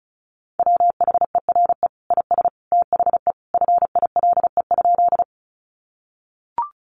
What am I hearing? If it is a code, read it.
Morse code, heard as a keyed tone, about 34 wpm, 700 Hz. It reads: WHERE IS THE FILE?
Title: Sentences from Top 1200 Words 35wpm